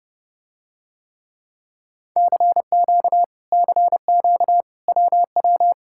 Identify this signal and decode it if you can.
Morse code, heard as a keyed tone, about 30 wpm, 700 Hz. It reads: CQ CQ WW